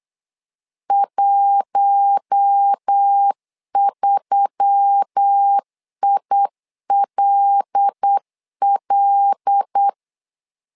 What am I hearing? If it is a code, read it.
Morse code, heard as a keyed tone, about 8 wpm, 800 Hz. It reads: 13ILL
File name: LEZIONI / 5 LBVC / 5 L 8wpm 2831